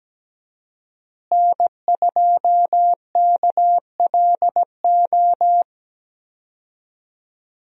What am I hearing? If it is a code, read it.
Morse code, heard as a keyed tone, about 17 wpm, 700 Hz. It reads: N2KLO
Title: USA Call Signs - 5 Characters - Repeated 3x - 17wpm